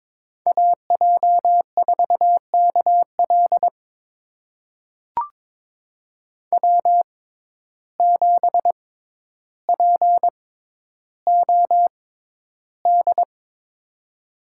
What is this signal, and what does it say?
Morse code, heard as a keyed tone, about 22 wpm, spaced as for 11 wpm, 700 Hz. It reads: AJ4KL W7POD
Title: USA Call Signs - 5 Characters - 6x Character Spacing - 22wpm